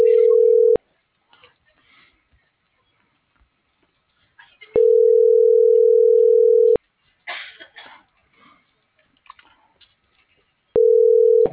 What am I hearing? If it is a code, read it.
Ringback tone (North America), 440 + 480 Hz.